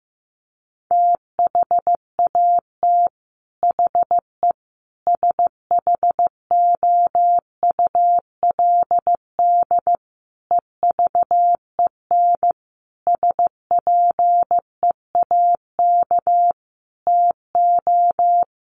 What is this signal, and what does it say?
Morse code, heard as a keyed tone, about 14 wpm, 700 Hz. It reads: THAT HE SHOULD EVEN SPEAK TO
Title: Sentences from Top 1500 Words 15wpm